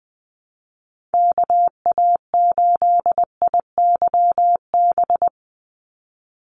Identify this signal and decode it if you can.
Morse code, heard as a keyed tone, about 20 wpm, 700 Hz. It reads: KA8IYB
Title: USA Call Signs 20wpm